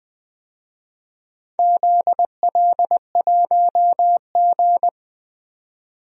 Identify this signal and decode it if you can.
Morse code, heard as a keyed tone, about 20 wpm, 700 Hz. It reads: ZL1G
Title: DXCC-100 Call Signs - 4 Characters - 20wpm